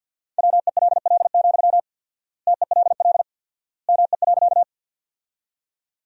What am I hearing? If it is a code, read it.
Morse code, heard as a keyed tone, about 50 wpm, 700 Hz. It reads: WELL, TELL ME.
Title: 4x Word-Spacing - Sentences from Top 200 Words - 50wpm